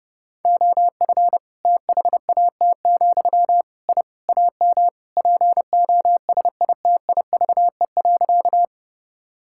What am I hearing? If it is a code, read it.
Morse code, heard as a keyed tone, about 30 wpm, 700 Hz. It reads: OF THAT, I AM POSITIVE.
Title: Sentences from Top 1000 Words - Review Part 5 - 30wpm